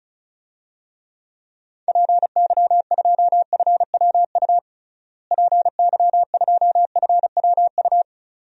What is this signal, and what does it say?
Morse code, heard as a keyed tone, about 34 wpm, 700 Hz. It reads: PY2FWU PY2FWU
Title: DXCC-100 Call Signs - 6 Characters - Repeated 2x - 35wpm